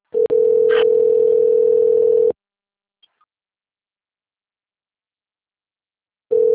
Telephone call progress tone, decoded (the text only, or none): ringback tone